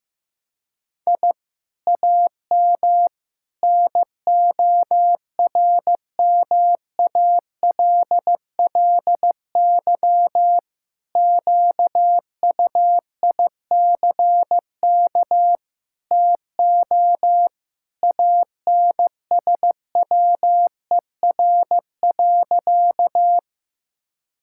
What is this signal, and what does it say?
Morse code, heard as a keyed tone, about 15 wpm, 700 Hz. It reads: I AM NORMALLY QUICK TO ANSWER.
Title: Sentences from Top 1000 Words - Review Part 6 - 15wpm